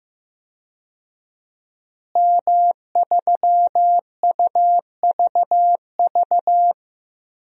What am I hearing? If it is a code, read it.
Morse code, heard as a keyed tone, about 14 wpm, 700 Hz. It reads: M3UVV
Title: DXCC-100 Call Signs - 5 Characters - Repeated 3x - 15wpm